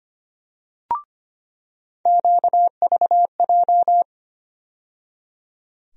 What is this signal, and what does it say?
Morse code, heard as a keyed tone, about 25 wpm, 700 Hz. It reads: QVJ
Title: USA Call Sign Postfixes - 3 Letters 25wpm